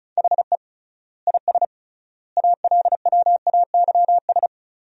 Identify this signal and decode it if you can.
Morse code, heard as a keyed tone, about 36 wpm, 700 Hz. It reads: HE IS ALWAYS
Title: 3x Word-Spacing - Sentences from Top 300 Words - 35wpm